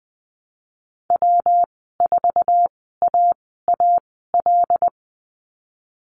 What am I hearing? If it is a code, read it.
Morse code, heard as a keyed tone, about 20 wpm, spaced as for 15 wpm, 700 Hz. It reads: W4AAL
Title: 128 Lesson 21g - Number 6 - Call Signs 20wpm